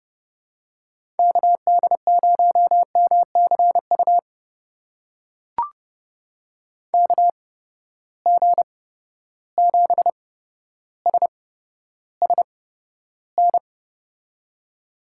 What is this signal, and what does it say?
Morse code, heard as a keyed tone, about 30 wpm, spaced as for 13 wpm, 700 Hz. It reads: KD0MCU KG7SSN
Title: USA Call Signs - 6 Characters - 8x Character Spacing - 30wpm